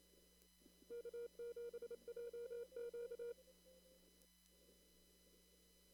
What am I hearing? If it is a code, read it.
Morse code, heard as a keyed tone, about 28 wpm, 470 Hz. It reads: K7JQ
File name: CW / kt5j wpx cw-op-k5pi